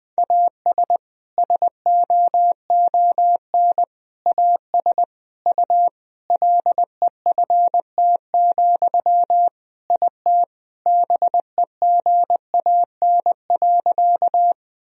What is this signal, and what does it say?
Morse code, heard as a keyed tone, about 20 wpm, 700 Hz. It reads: AS SOON AS U LEFT, IT BEGAN.
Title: Sentences from Top 400 Words and Abbreviations 20wpm